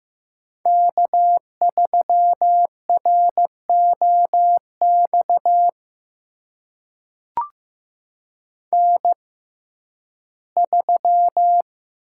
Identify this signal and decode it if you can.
Morse code, heard as a keyed tone, about 15 wpm, spaced as for 11 wpm, 700 Hz. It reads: K3ROX N3
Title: USA Call Signs - 5 Characters - 6x Character Spacing - 15wpm